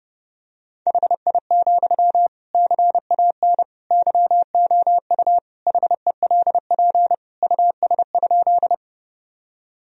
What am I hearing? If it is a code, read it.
Morse code, heard as a keyed tone, about 31 wpm, 700 Hz. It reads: HI, CAN YOU HELP US?
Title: Sentences from Top 1000 Words - Review Part 3 - 30wpm